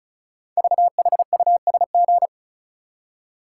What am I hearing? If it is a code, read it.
Morse code, heard as a keyed tone, about 36 wpm, 690 Hz. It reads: VHUSG
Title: Code Groups - 5 Letters - 35wpm